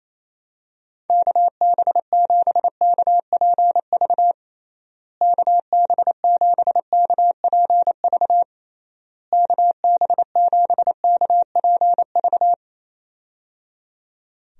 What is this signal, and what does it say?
Morse code, heard as a keyed tone, about 28 wpm, 700 Hz. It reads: KB7KPV KB7KPV KB7KPV
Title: USA Call Signs - 6 Characters - Repeated 3x - 28wpm